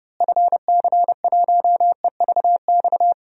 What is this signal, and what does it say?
Morse code, heard as a keyed tone, about 30 wpm, 700 Hz. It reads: FC1EVX
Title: DXCC-100 Call Signs - 6 Characters - Repeated 3x - 30wpm